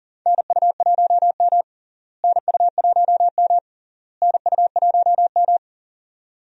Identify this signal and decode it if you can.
Morse code, heard as a keyed tone, about 40 wpm, 700 Hz. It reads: NU1M NU1M NU1M
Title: USA Call Signs - 4 Characters - Repeated 3x - 40wpm